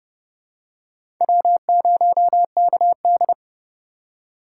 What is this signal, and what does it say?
Morse code, heard as a keyed tone, about 30 wpm, 700 Hz. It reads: W0KD
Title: USA Call Signs - 4 Characters - 2x Character Spacing - 30wpm